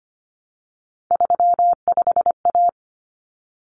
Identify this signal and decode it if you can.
Morse code, heard as a keyed tone, about 25 wpm, 700 Hz. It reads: 35A
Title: Common Cut Numbers - 3-Digit 25wpm